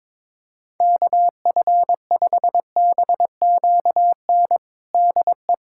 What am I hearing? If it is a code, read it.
Morse code, heard as a keyed tone, about 22 wpm, 700 Hz. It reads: KF5BQN DE
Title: USA Callsign DE Callsign - 6 Characters - 22wpm